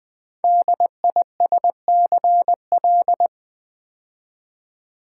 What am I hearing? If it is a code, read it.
Morse code, heard as a keyed tone, about 20 wpm, 700 Hz. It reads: DISCL
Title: Build A Word - 5 Letters At A Time - 10-Letter Words - 20wpm